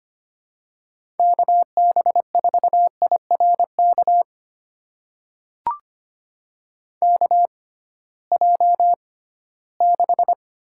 Morse code, heard as a keyed tone, about 25 wpm, spaced as for 16 wpm, 700 Hz. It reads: KB4IRK KJ6
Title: USA Call Signs - 6 Characters - 6x Character Spacing - 25wpm